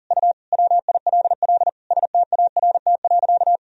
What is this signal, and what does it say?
Morse code, heard as a keyed tone, about 40 wpm, 700 Hz. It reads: U WILL START.
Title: Sentences from Top 300 Words and Abbreviations 40wpm